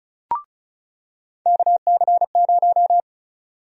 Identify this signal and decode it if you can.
Morse code, heard as a keyed tone, about 36 wpm, 700 Hz. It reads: KC0
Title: USA Call Sign Prefixes - 35wpm